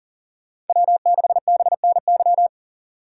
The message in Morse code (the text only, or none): W6BNY